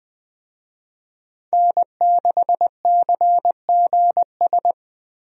Morse code, heard as a keyed tone, about 21 wpm, 700 Hz. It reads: N6CGS
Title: USA Call Signs - 5 Characters - Repeated 3x - 20wpm